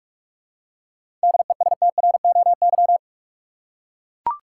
Morse code, heard as a keyed tone, about 43 wpm, 700 Hz. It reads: DESTROY
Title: Seven Letter Words - 45wpm